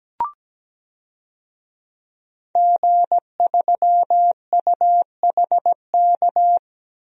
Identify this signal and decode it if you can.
Morse code, heard as a keyed tone, about 17 wpm, 700 Hz. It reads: G3UHK
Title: DXCC-100 Call Signs - 5 Characters - 17wpm